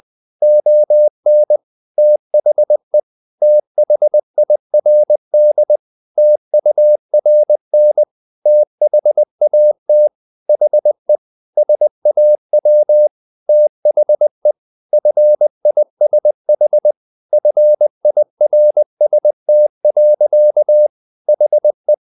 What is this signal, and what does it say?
Morse code, heard as a keyed tone, about 20 wpm, 600 Hz. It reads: ON THE THIRD TURN THAT HE SAW THE FISH FIRST. HE
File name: the old man and the sea part 2 20wpm